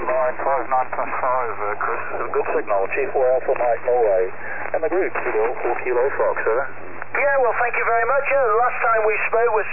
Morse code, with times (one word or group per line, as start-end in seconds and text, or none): none